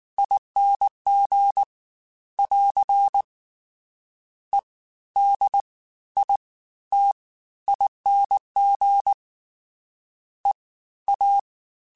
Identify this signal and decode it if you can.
Morse code, heard as a keyed tone, about 19 wpm, spaced as for 13 wpm, 800 Hz. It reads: ING <AR> EDITING EA
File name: Special INGs-mp3s / Special INGs-mp3s / Fast ing @18